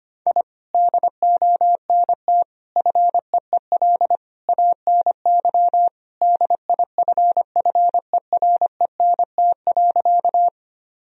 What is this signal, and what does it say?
Morse code, heard as a keyed tone, about 25 wpm, 700 Hz. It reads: I DONT FEEL ANY DIFFERENT.
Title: Sentences from Top 1000 Words - Review Part 3 - 25wpm